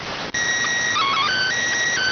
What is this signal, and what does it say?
SSTV (slow-scan television), PD 290.